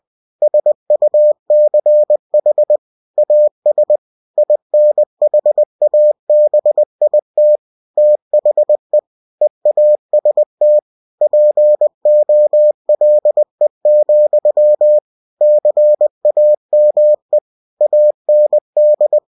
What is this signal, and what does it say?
Morse code, heard as a keyed tone, about 20 wpm, 600 Hz. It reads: SUCH AS INHABIT THE EAST POLE, CAME AND
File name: cw-books / winnie-the-pooh / winnie-20wpm-page-0094